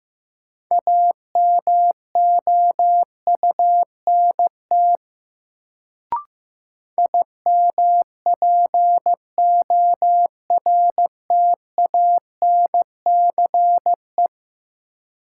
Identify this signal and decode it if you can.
Morse code, heard as a keyed tone, about 15 wpm, 700 Hz. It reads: AMOUNT IMPORTANCE
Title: Sets of 3 Words - New Word Drill - 15wpm